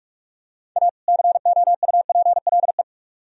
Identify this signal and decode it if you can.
Morse code, heard as a keyed tone, about 40 wpm, 700 Hz. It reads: A COUPLE